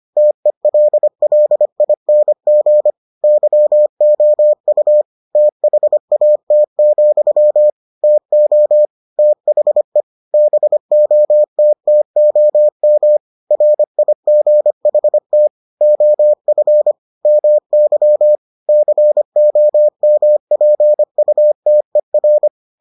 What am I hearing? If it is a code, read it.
Morse code, heard as a keyed tone, about 25 wpm, 600 Hz. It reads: TELLING YOU THAT, TO THE BOTTOM RIGHT OF MY COMPUTER